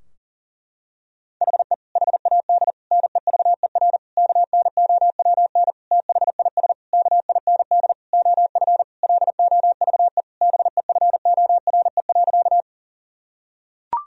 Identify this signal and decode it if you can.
Morse code, heard as a keyed tone, about 40 wpm, 700 Hz. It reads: HE HAD NEVER KNOWN THIS KIND OF LOVE BEFORE.
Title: Sentences from Top 1000 Words - Review Part 3 - 40wpm